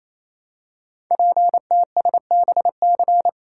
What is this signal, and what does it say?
Morse code, heard as a keyed tone, about 29 wpm, 700 Hz. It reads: PTSBC
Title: Code Groups - 5 Letters - 28wpm